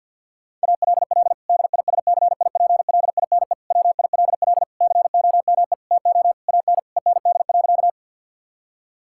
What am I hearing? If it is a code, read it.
Morse code, heard as a keyed tone, about 50 wpm, 700 Hz. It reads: ALL DISCIPLINE WILL COME TO AN END.
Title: Sentences from Top 1000 Words - Review Part 6 - 50wpm